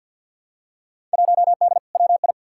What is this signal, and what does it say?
Morse code, heard as a keyed tone, about 51 wpm, 700 Hz. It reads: 1D WI
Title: Field Day Exchanges - 50wpm